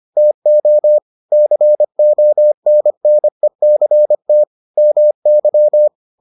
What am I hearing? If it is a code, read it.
Morse code, heard as a keyed tone, about 25 wpm, 600 Hz. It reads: TO CONNECT MY